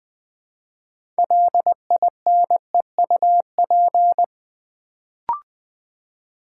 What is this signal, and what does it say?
Morse code, heard as a keyed tone, about 20 wpm, 700 Hz. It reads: LINEUP
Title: Build A Word - 2 Letters At A Time - 6-Letter Words - 20wpm